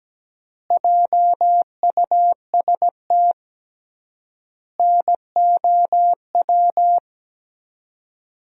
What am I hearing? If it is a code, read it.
Morse code, heard as a keyed tone, about 17 wpm, 700 Hz. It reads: JUST NOW
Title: 3x Word-Spacing - Sentences from Top 300 Words - 17wpm